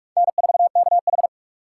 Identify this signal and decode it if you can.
Morse code, heard as a keyed tone, about 41 wpm, 700 Hz. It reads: N4KH
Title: K1USN SST QSO Practice - 45wpm